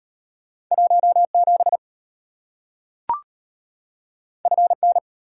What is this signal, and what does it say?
Morse code, heard as a keyed tone, about 33 wpm, 700 Hz. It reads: 17 FN